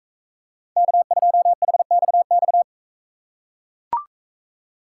Morse code, heard as a keyed tone, about 40 wpm, 700 Hz. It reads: K2HXX